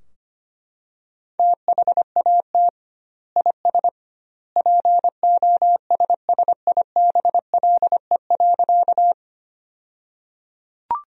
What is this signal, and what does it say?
Morse code, heard as a keyed tone, about 25 wpm, 700 Hz. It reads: THAT IS POSSIBLE.
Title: 2x Word-Spacing - Sentences from Top 500 Words - 25wpm